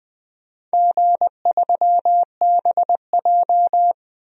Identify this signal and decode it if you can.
Morse code, heard as a keyed tone, about 20 wpm, 700 Hz. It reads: G3BJ
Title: DXCC-100 Call Signs - 4 Characters - Repeated 2x - 20wpm